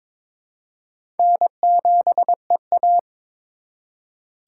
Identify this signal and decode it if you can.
Morse code, heard as a keyed tone, about 22 wpm, 700 Hz. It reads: N7EA
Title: USA Call Signs - 3x Character Spacing - 22wpm